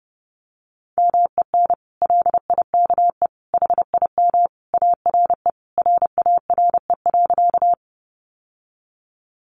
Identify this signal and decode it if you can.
Morse code, heard as a keyed tone, about 30 wpm, 700 Hz. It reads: MEN LIKE HIM ARE RARE.
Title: Sentences from Top 1100 Words 30wpm